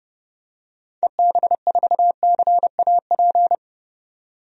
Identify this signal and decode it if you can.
Morse code, heard as a keyed tone, about 30 wpm, 690 Hz. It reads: EB4CAP
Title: DXCC-100 Call Signs - 6 Characters - 30wpm